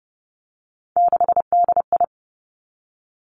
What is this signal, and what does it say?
Morse code, heard as a keyed tone, about 30 wpm, 700 Hz. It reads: 6DI
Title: Three Challenging Letter-Number Combinations 30wpm